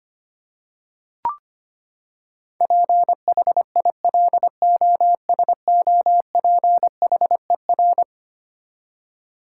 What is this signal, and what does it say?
Morse code, heard as a keyed tone, about 25 wpm, 700 Hz. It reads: PHILOSOPHER
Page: morse-code-ninja-practice-25wpm